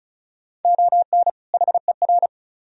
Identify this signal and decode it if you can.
Morse code, heard as a keyed tone, about 45 wpm, 700 Hz. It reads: ON HER